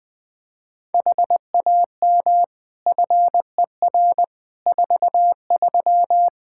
Morse code, heard as a keyed tone, about 20 wpm, 700 Hz. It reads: HAM FER 43
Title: K1USN SST QSO Practice 20wpm